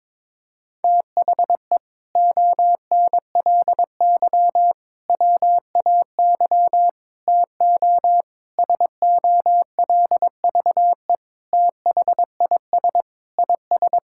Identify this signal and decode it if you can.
Morse code, heard as a keyed tone, about 22 wpm, 700 Hz. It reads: THE ONLY WAY TO SOLVE THIS IS
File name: practice / 22wpm / Sentences with 4-Word Phrases - 22wpm